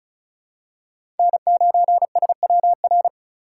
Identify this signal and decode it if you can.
Morse code, heard as a keyed tone, about 34 wpm, 700 Hz. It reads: N9SWR
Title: USA Call Signs - 5 Characters - 6x Character Spacing - 35wpm